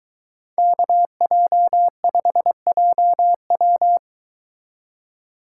Morse code, heard as a keyed tone, about 22 wpm, 700 Hz. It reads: KJ5JW